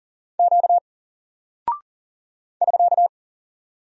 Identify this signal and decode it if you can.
Morse code, heard as a keyed tone, about 40 wpm, 700 Hz. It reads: Q <SK>
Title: Single Character (Letter:Number:Punctuation:Prosign) Alt 40wpm